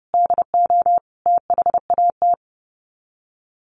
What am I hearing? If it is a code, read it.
Morse code, heard as a keyed tone, about 30 wpm, 700 Hz. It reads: DO THAT
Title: Sets of 5 Words 30wpm